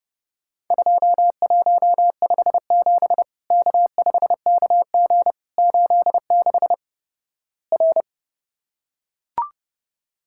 Morse code, heard as a keyed tone, about 32 wpm, 700 Hz. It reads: 2157 K5KG 86 R
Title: Worked All Europe DX Contest - QTC Exchange - 30wpm